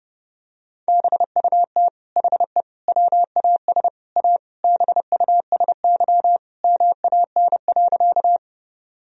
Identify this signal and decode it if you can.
Morse code, heard as a keyed tone, about 32 wpm, 700 Hz. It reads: BUT HE WAS A BUSY MAN.
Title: Sentences from Top 1000 Words - Review Part 3 - 30wpm